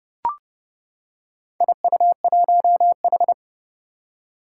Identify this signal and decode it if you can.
Morse code, heard as a keyed tone, about 30 wpm, 700 Hz. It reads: IU1H
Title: Four Challenging Letter-Number Combinations 30wpm